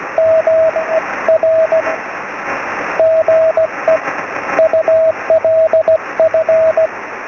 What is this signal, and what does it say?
Morse code, heard as a keyed tone, about 17 wpm, spaced as for 13 wpm, 640 Hz, with radio noise. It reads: MERGE ULF
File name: sm7lcb / 16-oz6ol